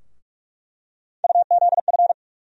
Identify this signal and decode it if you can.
Morse code, heard as a keyed tone, about 46 wpm, 700 Hz. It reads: UZF
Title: USA Call Sign Postfixes - 3 Letters 45wpm